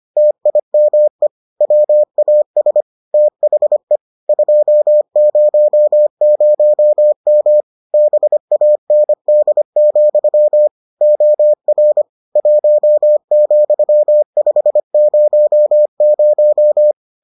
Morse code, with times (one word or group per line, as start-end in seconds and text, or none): 0.2-1.3 TIME
1.6-2.8 WAS
3.1-4.0 THE
4.3-7.6 200M
7.9-10.7 BAND,
11.0-12.0 OR
12.4-16.9 1,500